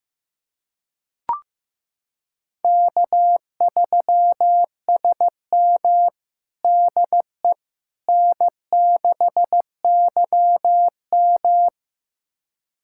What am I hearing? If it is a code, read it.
Morse code, heard as a keyed tone, about 15 wpm, 700 Hz. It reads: K3SM DE N6YM
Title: USA Callsign DE Callsign - 4 Characters - 15wpm